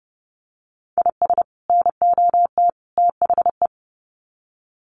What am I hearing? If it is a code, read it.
Morse code, heard as a keyed tone, about 30 wpm, 700 Hz. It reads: IS NOT THE
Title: Sets of 3 Words 30wpm